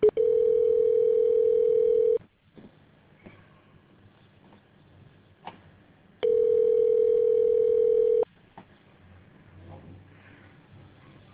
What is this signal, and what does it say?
Ringback tone (North America), 440 + 480 Hz.